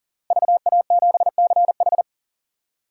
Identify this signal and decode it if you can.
Morse code, heard as a keyed tone, about 39 wpm, 700 Hz. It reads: VA7CH
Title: DXCC-100 Call Signs - 5 Characters - Repeated 2x - 40wpm